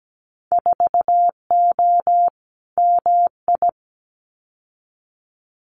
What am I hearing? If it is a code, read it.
Morse code, heard as a keyed tone, about 17 wpm, 700 Hz. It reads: 4O MI
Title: Winter Field Day Exchanges Encore 17wpm